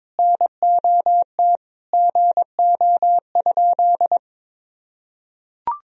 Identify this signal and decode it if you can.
Morse code, heard as a keyed tone, about 22 wpm, 700 Hz. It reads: NOT GO?
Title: Sentences from Top 1000 Words - Review Part 4 - 22wpm